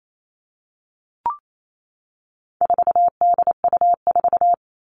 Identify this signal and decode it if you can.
Morse code, heard as a keyed tone, about 28 wpm, 700 Hz. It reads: 4DU4
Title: All Cut Numbers - 4-Digit 28wpm